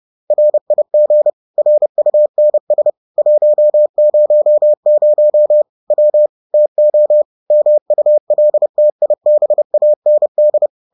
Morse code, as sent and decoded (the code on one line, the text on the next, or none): .-. .. --. / .-. ..- -. ... / .---- ----- ----- / .-- / - --- / -- ..- .-.. - .. -... .- -. -..
RIG RUNS 100 W TO MULTIBAND